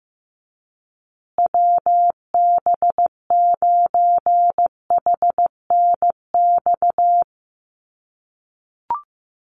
Morse code, heard as a keyed tone, about 15 wpm, 700 Hz. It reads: WB9HNX
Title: USA Call Signs Repeated 3x 15wpm